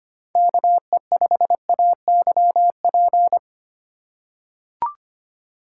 Morse code, heard as a keyed tone, about 25 wpm, 700 Hz. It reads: KE5AYP
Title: USA Call Signs - 3x Character Spacing - 25wpm